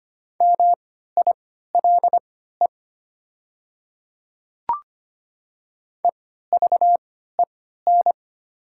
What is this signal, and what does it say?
Morse code, heard as a keyed tone, about 25 wpm, spaced as for 14 wpm, 700 Hz. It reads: MILE EVEN
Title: 3x Character Spacing - Top 300 Words - 25wpm